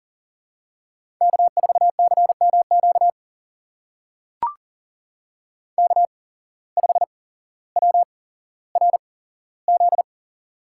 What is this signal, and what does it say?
Morse code, heard as a keyed tone, about 39 wpm, spaced as for 17 wpm, 700 Hz. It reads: K4CMQ K5WRZ
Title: USA Call Signs - 5 Characters - 8x Character Spacing - 40wpm